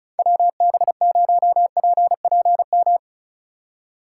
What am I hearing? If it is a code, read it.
Morse code, heard as a keyed tone, about 42 wpm, 700 Hz. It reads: WB0PPM
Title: Speed Racing - USA Call Signs 35wpm